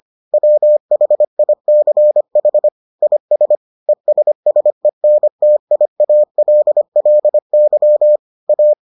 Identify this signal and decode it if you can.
Morse code, heard as a keyed tone, about 25 wpm, 600 Hz. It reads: WHICH IS ESSENTIALLY A